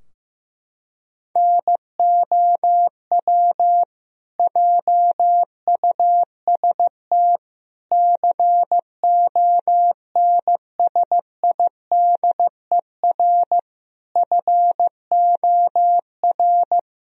Morse code, heard as a keyed tone, about 16 wpm, 700 Hz. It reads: NOW JUST CONSIDER FOR
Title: Sentences from Top 1000 Words - Review Part 4 - 15wpm